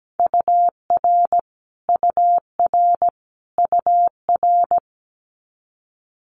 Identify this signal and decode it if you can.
Morse code, heard as a keyed tone, about 17 wpm, 700 Hz. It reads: UR UR UR
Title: Multi-letter Phonemes Repeated 3x 17wpm